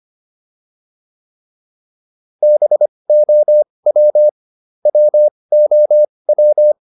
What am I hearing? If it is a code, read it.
Morse code, heard as a keyed tone, about 25 wpm, spaced as for 20 wpm, 600 Hz. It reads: BOW WOW